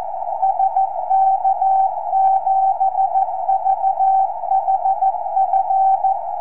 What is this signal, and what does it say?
Morse code, heard as a keyed tone, about 14 wpm, 770 Hz, with radio noise. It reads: SK7VHF
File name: tropo / sk7vhf